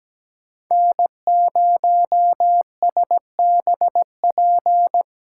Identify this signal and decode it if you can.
Morse code, heard as a keyed tone, about 17 wpm, 700 Hz. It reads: N0SBP
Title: USA Callsign DE Callsign - 5 Characters - 17wpm